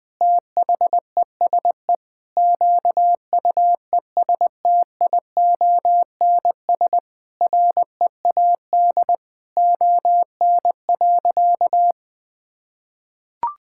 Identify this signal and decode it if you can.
Morse code, heard as a keyed tone, about 20 wpm, 700 Hz. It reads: THESE QUESTIONS READ ON.
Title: Sentences with 9-Word Phrases - 20wpm